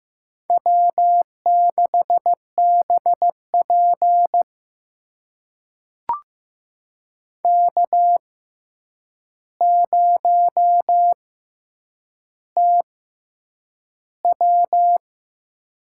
Morse code, heard as a keyed tone, about 15 wpm, spaced as for 8 wpm, 700 Hz. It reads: W6BP K0TW
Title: USA Call Signs - 4 Characters - 6x Character Spacing - 15wpm